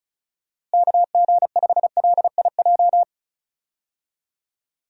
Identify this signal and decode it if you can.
Morse code, heard as a keyed tone, about 34 wpm, 700 Hz. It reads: KG5LIJ
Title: USA Call Signs - 6 Characters - Repeated 3x - 35wpm